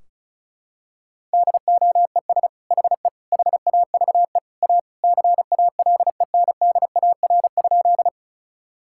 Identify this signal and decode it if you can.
Morse code, heard as a keyed tone, about 36 wpm, 700 Hz. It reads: DOES HE HAVE A CALENDAR?
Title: Sentences from Top 1600 Words 35wpm